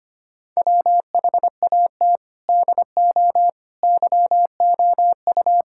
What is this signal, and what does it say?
Morse code, heard as a keyed tone, about 25 wpm, 700 Hz. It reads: WHAT DO YOU
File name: practice / 25wpm / Sentences with 4-Word Phrases - 25wpm